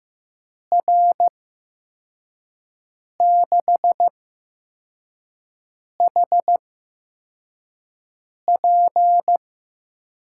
Morse code, heard as a keyed tone, about 15 wpm, spaced as for 4 wpm, 700 Hz. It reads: R6HP
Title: USA Call Signs - 5 Characters - 8x Character Spacing - 15wpm